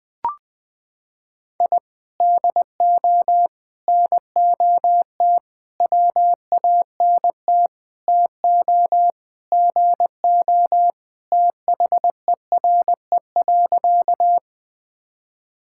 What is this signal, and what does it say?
Morse code, heard as a keyed tone, about 21 wpm, 700 Hz. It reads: I DO NOT WANT TO GO THERE.
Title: Sentences with 4-Word Phrases - 20wpm